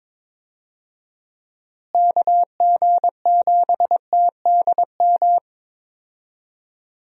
Morse code, heard as a keyed tone, about 22 wpm, 700 Hz. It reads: KG7TDM
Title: USA Call Signs - 6 Characters - Repeated 2x - 22wpm